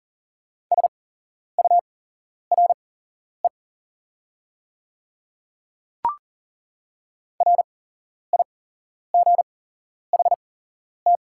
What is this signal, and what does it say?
Morse code, heard as a keyed tone, about 40 wpm, spaced as for 11 wpm, 690 Hz. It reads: SURE RIGHT
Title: 8x Character Spacing - Top 300 Words - 40wpm